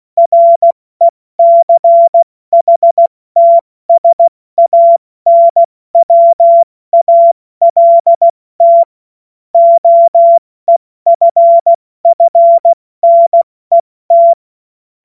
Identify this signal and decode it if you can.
Morse code, heard as a keyed tone, about 16 wpm, 680 Hz. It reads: RECHTSANWALT OEFFNET
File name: Morsekurs / Witze / Witz 17. Mai 2022 70 F80